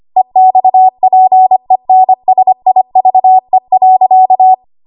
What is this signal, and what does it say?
Morse code, heard as a keyed tone, about 25 wpm, 750 Hz. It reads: EXPENSIVE.
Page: W1AW Web Code Practice Files